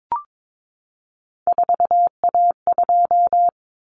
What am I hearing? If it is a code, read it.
Morse code, heard as a keyed tone, about 22 wpm, 700 Hz. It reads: 4A2
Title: Common Cut Numbers - 3-Digit 22wpm